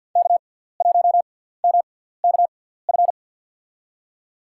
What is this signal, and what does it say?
Morse code, heard as a keyed tone, about 51 wpm, spaced as for 17 wpm, 700 Hz. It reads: K1MKF
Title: USA Call Signs - 5 Characters - 6x Character Spacing - 50wpm